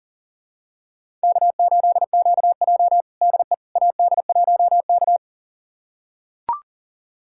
Morse code, heard as a keyed tone, about 40 wpm, 700 Hz. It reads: K8QJ DE AD1K
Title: USA Callsign DE Callsign - 4 Characters - 40wpm